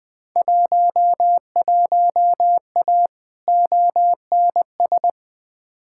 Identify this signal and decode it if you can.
Morse code, heard as a keyed tone, about 20 wpm, 700 Hz. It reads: 11A ONS
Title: Field Day Exchanges Encore - 20wpm